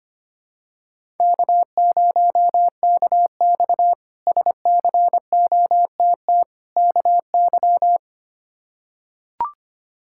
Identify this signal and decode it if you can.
Morse code, heard as a keyed tone, about 25 wpm, 700 Hz. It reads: K0KX SCOTT KY
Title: K1USN SST QSO Practice - 25wpm